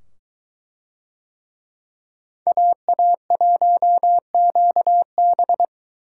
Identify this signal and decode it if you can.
Morse code, heard as a keyed tone, about 22 wpm, 700 Hz. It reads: AA1QB